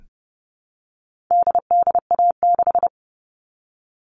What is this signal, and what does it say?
Morse code, heard as a keyed tone, about 30 wpm, 700 Hz. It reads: DDA6
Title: All Cut Numbers - 4-Digit 30wpm